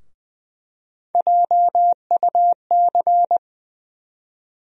Morse code, heard as a keyed tone, about 20 wpm, 700 Hz. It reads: JUC
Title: USA Call Sign Postfixes - 3 Letters 20wpm